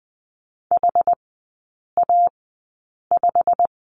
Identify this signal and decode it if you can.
Morse code, heard as a keyed tone, about 20 wpm, spaced as for 8 wpm, 700 Hz. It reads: HA5
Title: 127 Lesson 21e - Number 6 - Three Characters 20wpm